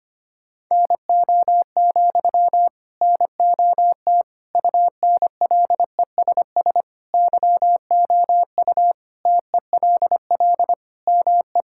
Morse code, heard as a keyed tone, about 25 wpm, 700 Hz. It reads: NO, NOT UNLESS YOU TELL ME
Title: Sentences from Top 1000 Words - Review Part 6 - 25wpm